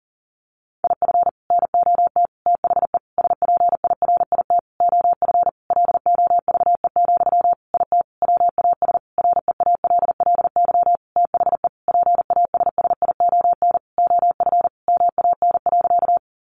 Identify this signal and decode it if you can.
Morse code, heard as a keyed tone, about 40 wpm, 700 Hz. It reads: IF NOT THE SPIRIT OF LOVE, IT WAS REALLY THE PASSION OF MAN.
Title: Sentences from Top 1400 Words 40wpm